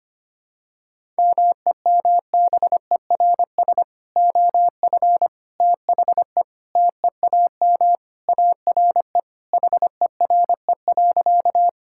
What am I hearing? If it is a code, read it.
Morse code, heard as a keyed tone, about 25 wpm, 700 Hz. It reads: MEMBERS OF THE TEAM ARE HERE.
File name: practice / 25wpm / Sentences with 3-Word Phrases - 25wpm